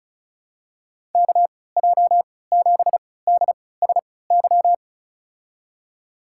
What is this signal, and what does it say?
Morse code, heard as a keyed tone, about 36 wpm, spaced as for 20 wpm, 700 Hz. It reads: KJ7DSY
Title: USA Call Signs - 6 Characters - 3x Character Spacing - 35wpm